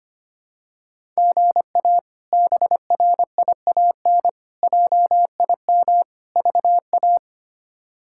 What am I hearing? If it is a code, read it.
Morse code, heard as a keyed tone, about 25 wpm, 700 Hz. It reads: GA BRIAN JIM VA
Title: K1USN SST QSO Practice - 25wpm